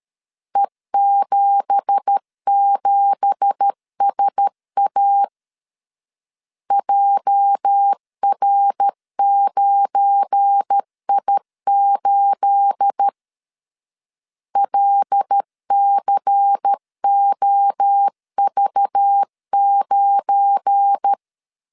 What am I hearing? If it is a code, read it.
Morse code, heard as a keyed tone, about 13 wpm, 800 Hz. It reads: E77SA JR9I8 LCOV9
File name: LEZIONI / 9 2867 12wpm 2322